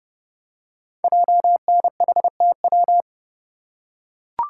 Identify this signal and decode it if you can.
Morse code, heard as a keyed tone, about 29 wpm, 700 Hz. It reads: JNHTW
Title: Code Groups - 5 Letters - 30wpm